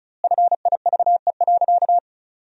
Morse code, heard as a keyed tone, about 36 wpm, 700 Hz. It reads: FIVE.
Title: 2x Word-Spacing - Sentences from Top 400 Words - 35wpm